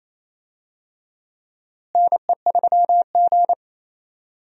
Morse code, heard as a keyed tone, about 28 wpm, 700 Hz. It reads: NE3G
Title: USA Call Signs - 4 Characters - Repeated 3x - 28wpm